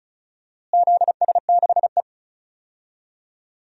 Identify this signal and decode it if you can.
Morse code, heard as a keyed tone, about 36 wpm, 700 Hz. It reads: ZS6E
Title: DXCC-100 Call Signs - 4 Characters - Repeated 2x - 35wpm